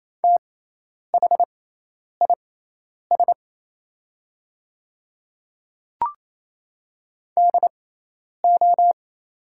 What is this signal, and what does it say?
Morse code, heard as a keyed tone, about 28 wpm, spaced as for 10 wpm, 690 Hz. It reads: THIS DO
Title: 6x Character Spacing - Top 100 Words - 28wpm